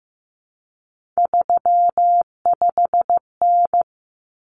Morse code, heard as a keyed tone, about 15 wpm, 700 Hz. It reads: 35N
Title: Common Cut Numbers - 3-Digit 15wpm